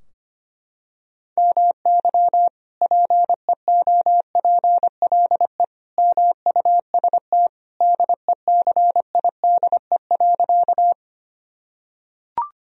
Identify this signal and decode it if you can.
Morse code, heard as a keyed tone, about 25 wpm, 700 Hz. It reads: MY PEOPLE MUST DECIDE.
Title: Sentences from Top 1000 Words - Review Part 3 - 25wpm